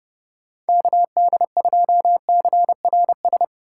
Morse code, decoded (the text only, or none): KD2CRS